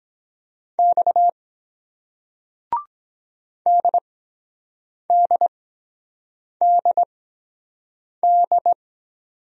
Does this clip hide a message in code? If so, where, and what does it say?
Yes: Morse code from 0.8 to 8.7 s, heard as a keyed tone, about 23 wpm, spaced as for 7 wpm, 700 Hz. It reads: X DDDD